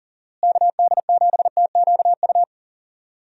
Morse code, heard as a keyed tone, about 40 wpm, 700 Hz. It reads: KD7TQU
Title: USA Call Signs - 6 Characters - Repeated 3x - 40wpm